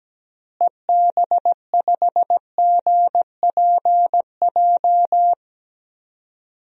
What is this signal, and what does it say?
Morse code, heard as a keyed tone, about 17 wpm, 700 Hz. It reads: EB5GPJ
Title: DXCC-100 Call Signs - 6 Characters - Repeated 2x - 17wpm